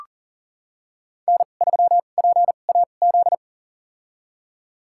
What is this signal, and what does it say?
Morse code, heard as a keyed tone, about 40 wpm, spaced as for 29 wpm, 700 Hz. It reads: N3PAZ
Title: USA Call Signs - 5 Characters - 2x Character Spacing - 40wpm